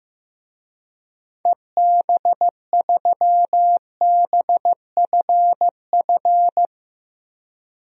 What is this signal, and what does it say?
Morse code, heard as a keyed tone, about 15 wpm, 700 Hz. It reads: EB3BFF
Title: DXCC-100 Call Signs - 6 Characters - Repeated 3x - 15wpm